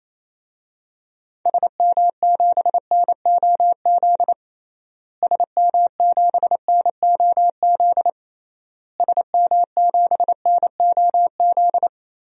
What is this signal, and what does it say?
Morse code, heard as a keyed tone, about 27 wpm, 700 Hz. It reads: SM7NOZ SM7NOZ SM7NOZ
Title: DXCC-100 Call Signs - 6 Characters - Repeated 3x - 28wpm